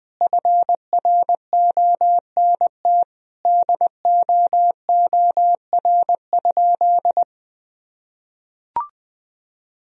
Morse code, heard as a keyed tone, about 21 wpm, 700 Hz. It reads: FRONT DOOR?
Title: Sentences from Top 1000 Words - Review Part 2 - 20wpm